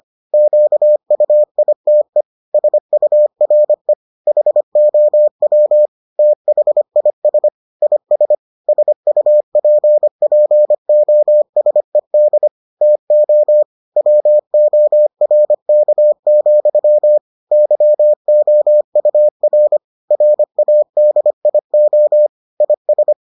Morse code, heard as a keyed tone, about 25 wpm, 600 Hz. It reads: QUITE SURE HOW THIS IS SUPPOSED TO WORK, YOUR RADIO IS